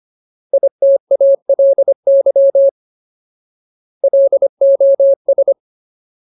Morse code, heard as a keyed tone, about 25 wpm, 550 Hz. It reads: ITALY LOS
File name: Ultimate Word List / 25WPM Ultimate